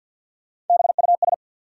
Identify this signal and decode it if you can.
Morse code, heard as a keyed tone, about 53 wpm, 700 Hz. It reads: BUS
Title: Sentences from Top 1000 Words - Review Part 6 - New Word Drill - 50wpm